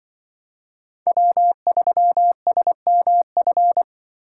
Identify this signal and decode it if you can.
Morse code, heard as a keyed tone, about 24 wpm, 700 Hz. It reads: W3SMF
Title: Speed Racing - USA Call Signs 20wpm